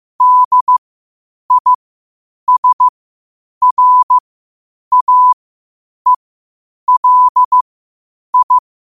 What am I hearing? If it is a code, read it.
Morse code, heard as a keyed tone, about 15 wpm, spaced as for 8 wpm, 1000 Hz. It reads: DISRAELI
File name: qotdfiles / 8wpm